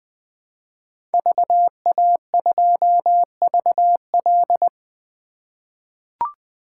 Morse code, heard as a keyed tone, about 21 wpm, 700 Hz. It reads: VA2VL
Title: 2023 SOTA Chaser Call Signs - 20wpm